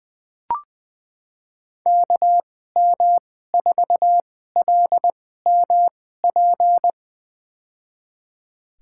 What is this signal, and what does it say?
Morse code, heard as a keyed tone, about 20 wpm, spaced as for 15 wpm, 700 Hz. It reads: KM4LMP
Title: USA Call Signs - 6 Characters - 2x Character Spacing - 20wpm